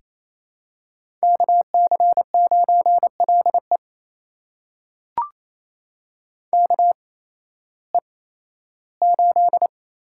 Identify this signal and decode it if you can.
Morse code, heard as a keyed tone, about 28 wpm, spaced as for 15 wpm, 700 Hz. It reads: KC9LE KE8
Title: USA Call Signs - 8x Character Spacing - 28wpm